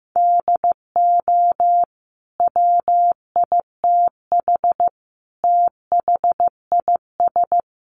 Morse code, heard as a keyed tone, about 15 wpm, 700 Hz. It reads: DO WITH THIS
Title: Sentences from Top 1400 Words 15wpm